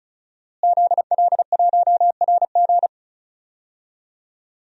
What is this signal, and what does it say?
Morse code, heard as a keyed tone, about 36 wpm, 700 Hz. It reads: ZL1RG
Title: DXCC-100 Call Signs - 5 Characters - Repeated 2x - 35wpm